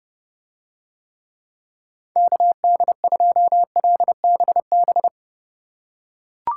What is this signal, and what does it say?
Morse code, heard as a keyed tone, about 30 wpm, 700 Hz. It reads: KD2LBB